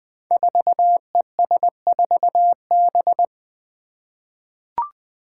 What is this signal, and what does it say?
Morse code, heard as a keyed tone, about 20 wpm, 700 Hz. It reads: 4ES4B
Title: Five Challenging Letter-Number Combinations 20wpm